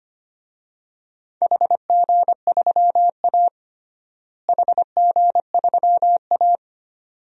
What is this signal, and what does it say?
Morse code, heard as a keyed tone, about 25 wpm, 700 Hz. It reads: HG3A HG3A
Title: DXCC-100 Call Signs - 4 Characters - Repeated 2x - 25wpm